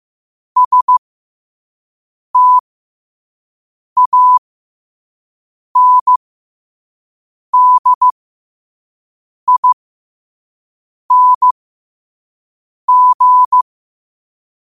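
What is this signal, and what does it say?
Morse code, heard as a keyed tone, about 15 wpm, spaced as for 5 wpm, 1000 Hz. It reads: STANDING
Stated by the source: Morse code sent at 5 wpm.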